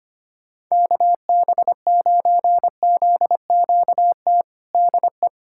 Morse code, heard as a keyed tone, about 25 wpm, 700 Hz. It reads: KB9ZQT DE
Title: USA Callsign DE Callsign 25wpm